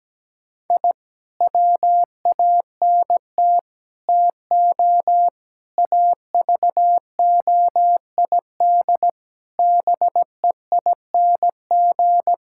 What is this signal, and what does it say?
Morse code, heard as a keyed tone, about 17 wpm, 700 Hz. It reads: I WANT TO AVOID BEING
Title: Sentences from Top 1000 Words - Review Part 6 - 17wpm